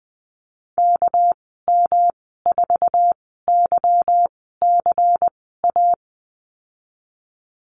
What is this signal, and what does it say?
Morse code, heard as a keyed tone, about 20 wpm, spaced as for 15 wpm, 700 Hz. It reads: KM4YCA
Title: 207 Lesson 33g - Letter j - Call Signs 20wpm